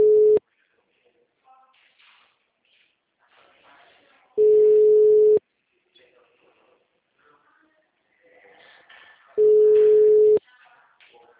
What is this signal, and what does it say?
Ringback tone (Europe, 425 Hz), with speech.